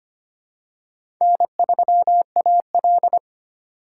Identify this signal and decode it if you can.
Morse code, heard as a keyed tone, about 25 wpm, 700 Hz. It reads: N3AL
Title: USA Call Signs - 4 Characters - 4x Character Spacing - 25wpm